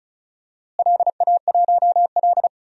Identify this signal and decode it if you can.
Morse code, heard as a keyed tone, about 36 wpm, 700 Hz. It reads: LA1L
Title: DXCC-100 Call Signs - 4 Characters - Repeated 3x - 35wpm